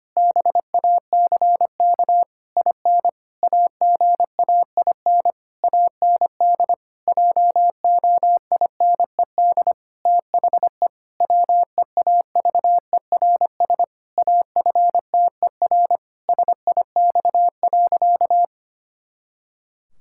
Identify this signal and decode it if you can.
Morse code, heard as a keyed tone, about 25 wpm, 700 Hz. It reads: BACK IN AGAIN AND JOINED THE WEAVERS AFTER SIX.
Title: Pangram Challenge v2 - 25wpm